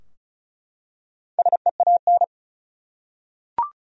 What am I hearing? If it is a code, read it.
Morse code, heard as a keyed tone, about 34 wpm, 700 Hz. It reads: SEAN
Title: USA Common Names - 35wpm